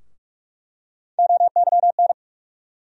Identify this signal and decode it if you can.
Morse code, heard as a keyed tone, about 42 wpm, 700 Hz. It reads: OYN
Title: USA Call Sign Postfixes - 3 Letters 45wpm